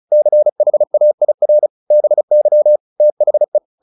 Morse code, heard as a keyed tone, about 34 wpm, 600 Hz. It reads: CHAIR BY THE
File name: the old man and the sea / the old man and the sea 1 35wpm